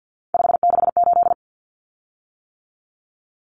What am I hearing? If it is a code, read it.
Morse code, heard as a keyed tone, about 48 wpm, 700 Hz. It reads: 568
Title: Common Cut Numbers - 3-Digit 50wpm